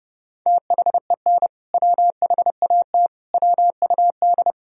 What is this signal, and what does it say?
Morse code, heard as a keyed tone, about 32 wpm, 700 Hz. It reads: THEN WHAT WUD
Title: Sentences from Top 400 Words and Abbreviations 30wpm